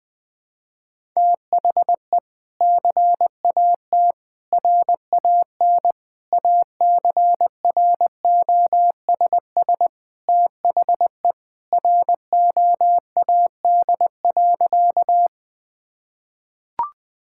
Morse code, heard as a keyed tone, about 20 wpm, 700 Hz. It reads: THE CAT RAN ACROSS THE ROAD.
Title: Sentences with 2-Word Phrases - 20wpm